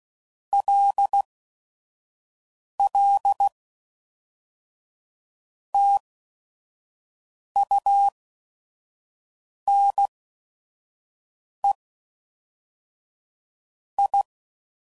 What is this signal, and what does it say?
Morse code, heard as a keyed tone, about 16 wpm, spaced as for 5 wpm, 800 Hz. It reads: LL TUNE I
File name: CW Code Groups @5WPM / QSOs / QSO WD0FBT